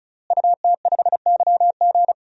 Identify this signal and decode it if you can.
Morse code, heard as a keyed tone, about 34 wpm, 700 Hz. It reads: UT5YG
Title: DXCC-100 Call Signs - 5 Characters - Repeated 3x - 35wpm